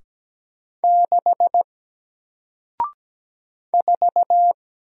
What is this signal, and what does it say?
Morse code, heard as a keyed tone, about 17 wpm, 700 Hz. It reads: 6 4
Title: Single Character (Letter:Number:Punctuation:Prosign) Alt 17wpm